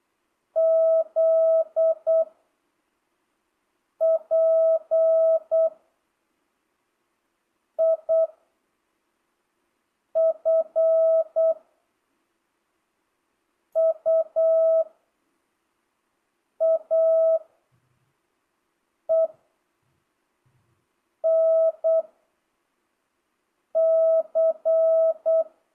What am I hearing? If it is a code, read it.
Morse code, heard as a keyed tone, about 8 wpm, 660 Hz. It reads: Z P I F U A E N C
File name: jh8cbh / 2025cw